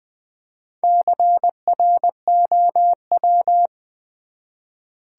Morse code, heard as a keyed tone, about 20 wpm, 700 Hz. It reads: CROW
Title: Build A Word - 1 Letter At A Time - 5-Letter Words - 20wpm